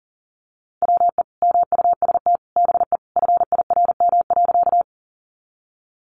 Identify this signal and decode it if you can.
Morse code, heard as a keyed tone, about 40 wpm, 700 Hz. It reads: WE MUST BE FIRM.
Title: Sentences from Top 1400 Words 40wpm